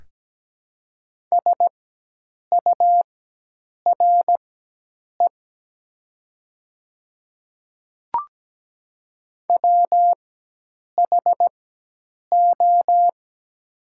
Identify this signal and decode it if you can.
Morse code, heard as a keyed tone, about 17 wpm, spaced as for 8 wpm, 700 Hz. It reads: SURE WHO
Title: 4x Character Spacing - Top 500 Words - 17wpm